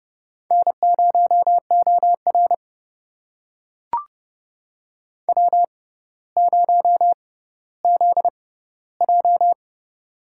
Morse code, heard as a keyed tone, about 29 wpm, spaced as for 15 wpm, 700 Hz. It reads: N0OR W0ZJ
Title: USA Call Signs - 4 Characters - 6x Character Spacing - 30wpm